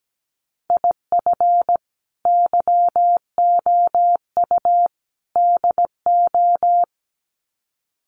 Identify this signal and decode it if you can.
Morse code, heard as a keyed tone, about 17 wpm, 700 Hz. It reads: IF YOU DO
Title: Sets of 3 Words 17wpm